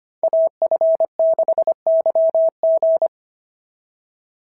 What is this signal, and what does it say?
Morse code, heard as a keyed tone, about 25 wpm, 690 Hz. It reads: AF6YG
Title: SOTA QSO Practice 25wpm